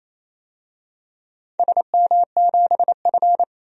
Morse code, heard as a keyed tone, about 27 wpm, 700 Hz. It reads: SM7F
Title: DXCC-100 Call Signs - 4 Characters - 28wpm